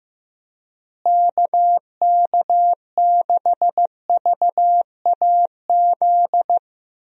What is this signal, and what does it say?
Morse code, heard as a keyed tone, about 15 wpm, 700 Hz. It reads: KK6VAZ
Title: USA Call Signs - 6 Characters - Repeated 3x - 15wpm